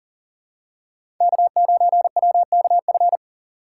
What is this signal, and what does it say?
Morse code, heard as a keyed tone, about 40 wpm, 700 Hz. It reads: K9WKF
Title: USA Call Signs - 5 Characters - 2x Character Spacing - 40wpm